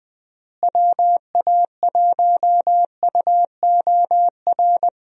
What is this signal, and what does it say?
Morse code, heard as a keyed tone, about 20 wpm, 700 Hz. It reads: WA1UOR
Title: USA Call Signs - 6 Characters - Repeated 3x - 20wpm